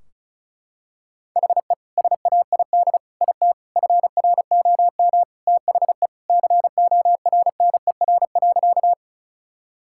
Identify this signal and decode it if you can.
Morse code, heard as a keyed tone, about 34 wpm, 700 Hz. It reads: HE SAID IT FROM THE CORNER.
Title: Sentences from Top 1600 Words 35wpm